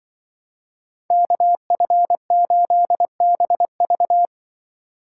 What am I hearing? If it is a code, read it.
Morse code, heard as a keyed tone, about 24 wpm, 700 Hz. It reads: KF8BV
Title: Speed Racing - USA 2x2 Call Signs 20wpm